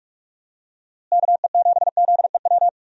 Morse code, heard as a keyed tone, about 46 wpm, 700 Hz. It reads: KE7ZEW
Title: USA Call Signs - 6 Characters - Repeated 3x - 45wpm